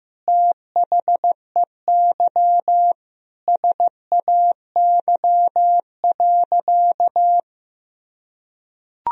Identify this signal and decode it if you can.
Morse code, heard as a keyed tone, about 15 wpm, 700 Hz. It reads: THEY SAY.
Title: Sentences from Top 1000 Words - Review Part 1 - 15wpm